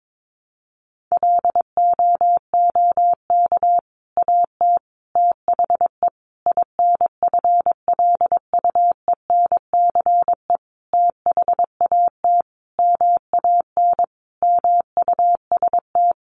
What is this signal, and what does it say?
Morse code, heard as a keyed tone, about 22 wpm, 700 Hz. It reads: LOOK AT THE INFLUENCE THAT MAN MUST
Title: Sentences from Top 1100 Words 22wpm